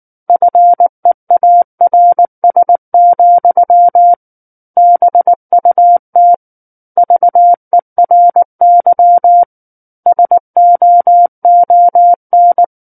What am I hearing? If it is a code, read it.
Morse code, heard as a keyed tone, about 19 wpm, 700 Hz. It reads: FEARS, BUT VERY SOON
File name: Book One / 10-Chapter 10 - 19 WPM